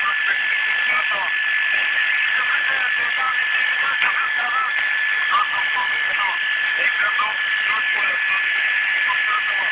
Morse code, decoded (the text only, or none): none